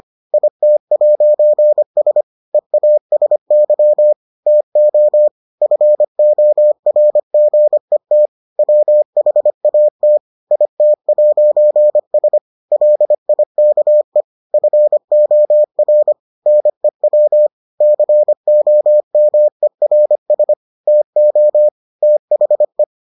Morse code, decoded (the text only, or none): IT'S EASY TO FORGET WHAT IT'S LIKE FOR NEW COMERS TO THE